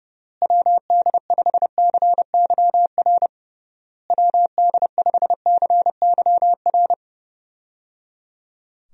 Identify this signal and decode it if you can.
Morse code, heard as a keyed tone, about 30 wpm, 700 Hz. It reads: WD5CYR WD5CYR
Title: USA Call Signs - 6 Characters - Repeated 3x - 30wpm